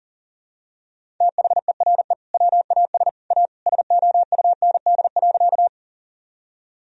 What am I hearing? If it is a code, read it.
Morse code, heard as a keyed tone, about 40 wpm, 700 Hz. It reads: THERE WAS A SOUND.
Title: Sentences from Top 1000 Words - Review Part 1 - 40wpm